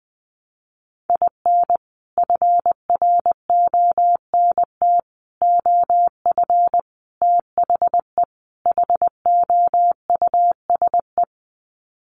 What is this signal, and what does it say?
Morse code, heard as a keyed tone, about 20 wpm, 700 Hz. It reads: IN FRONT OF THE HOUSE
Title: Sets of 5 Words Encore2 20wpm